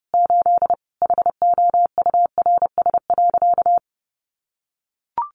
Morse code, heard as a keyed tone, about 30 wpm, 700 Hz. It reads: 8 HOURS.
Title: Additional Easy Phrases 30wpm